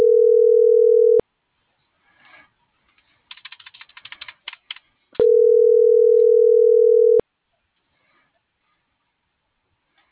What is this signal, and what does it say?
Ringback tone (North America), 440 + 480 Hz.